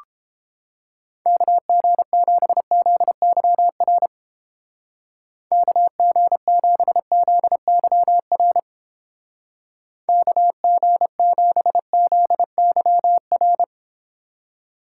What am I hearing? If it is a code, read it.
Morse code, heard as a keyed tone, about 31 wpm, 700 Hz. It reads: KG7ZYR KG7ZYR KG7ZYR